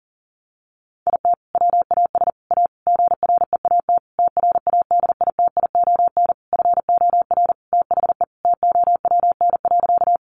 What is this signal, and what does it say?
Morse code, heard as a keyed tone, about 40 wpm, 700 Hz. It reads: IT WAS A GREAT TRADITION FOR THE TOWN.
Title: Sentences from Top 1500 Words 40wpm